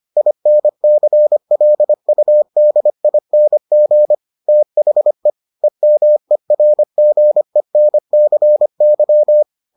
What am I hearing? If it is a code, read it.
Morse code, heard as a keyed tone, about 25 wpm, 600 Hz. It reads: INCLUDING THE EMERGENCY